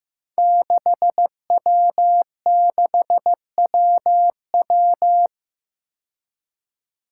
Morse code, heard as a keyed tone, about 15 wpm, 700 Hz. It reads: 6W6WW
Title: Five Challenging Letter-Number Combinations 15wpm